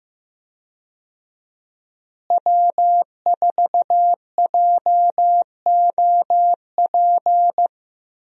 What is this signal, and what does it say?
Morse code, heard as a keyed tone, about 15 wpm, 700 Hz. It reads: W4JOP
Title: USA Call Signs - 5 Characters - Repeated 3x - 15wpm